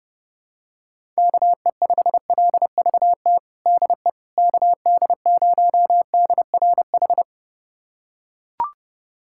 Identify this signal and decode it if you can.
Morse code, heard as a keyed tone, about 32 wpm, 700 Hz. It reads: KE5LVT DE KD0DRH
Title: USA Callsign DE Callsign - 6 Characters - 30wpm